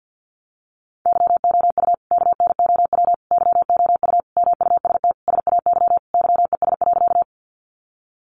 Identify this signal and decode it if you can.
Morse code, heard as a keyed tone, about 50 wpm, 700 Hz. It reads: YOU KNOW YOU MUST SAY YES.
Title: Sentences from Top 500 Words 50wpm